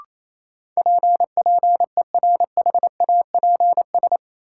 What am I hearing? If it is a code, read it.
Morse code, heard as a keyed tone, about 28 wpm, 700 Hz. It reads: P PERHAPS
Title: Sentences from Top 1100 Words - New Word Drill Alt - 28wpm